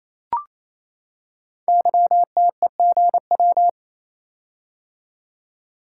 Morse code, heard as a keyed tone, about 27 wpm, 700 Hz. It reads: YTEGW
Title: Code Groups - 5 Letter:Numbers - 28wpm